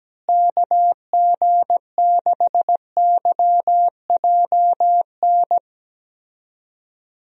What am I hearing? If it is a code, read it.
Morse code, heard as a keyed tone, about 17 wpm, 700 Hz. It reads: KG6YJN
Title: USA Call Signs - 6 Characters - Repeated 2x - 17wpm